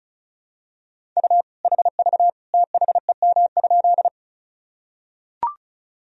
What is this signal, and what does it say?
Morse code, heard as a keyed tone, about 36 wpm, 700 Hz. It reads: U HV THEM?
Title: Sentences from Top 100 Words and Abbreviations 35wpm